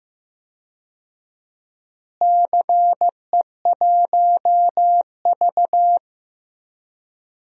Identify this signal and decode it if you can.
Morse code, heard as a keyed tone, about 15 wpm, 700 Hz. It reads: CE1V
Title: DXCC-100 Call Signs - 4 Characters - Repeated 2x - 15wpm